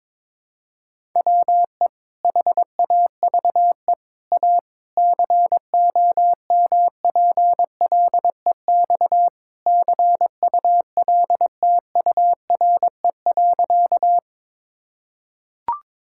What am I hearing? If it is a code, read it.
Morse code, heard as a keyed tone, about 21 wpm, 700 Hz. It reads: WE HAVE A COMPLEX CULTURE.
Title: Sentences from Top 1000 Words - Review Part 6 - 22wpm